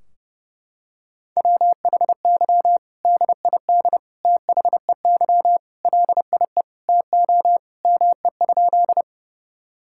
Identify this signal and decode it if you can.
Morse code, heard as a keyed tone, about 30 wpm, 700 Hz. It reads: WHY DID THEY LIE TO ME?
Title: Sentences from Top 1600 Words 30wpm